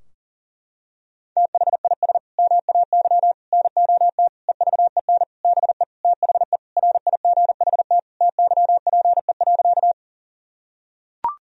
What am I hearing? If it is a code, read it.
Morse code, heard as a keyed tone, about 40 wpm, 700 Hz. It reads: THIS MAY NOT EVEN BE THE RIGHT TYPE.
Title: Sentences from Top 1000 Words - Review Part 3 - 40wpm